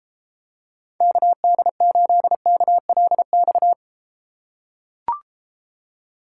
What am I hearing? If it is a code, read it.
Morse code, heard as a keyed tone, about 33 wpm, 700 Hz. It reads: KD8KLX